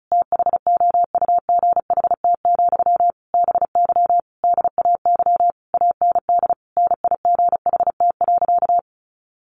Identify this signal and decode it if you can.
Morse code, heard as a keyed tone, about 34 wpm, 700 Hz. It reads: THOUGHT, BY DAY AND NIGHT.
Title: Sentences from Top 1500 Words 35wpm